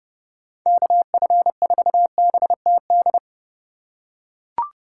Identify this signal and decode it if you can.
Morse code, heard as a keyed tone, about 32 wpm, 700 Hz. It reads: KF4BTD
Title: USA Call Signs - 6 Characters - 4x Character Spacing - 30wpm